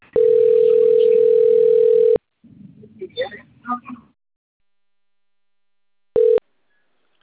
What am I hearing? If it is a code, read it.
Ringback tone (North America), 440 + 480 Hz, with speech.
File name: out- - - - - .5595